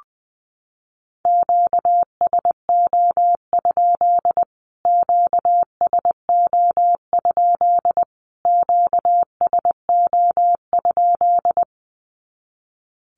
Morse code, heard as a keyed tone, about 19 wpm, 700 Hz. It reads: QSO? QSO? QSO?
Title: Q-Codes Repeated 3x 20wpm